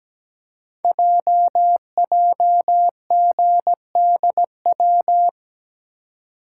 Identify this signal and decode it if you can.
Morse code, heard as a keyed tone, about 17 wpm, 700 Hz. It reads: JJGDW
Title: Code Groups - 5 Letters - 17wpm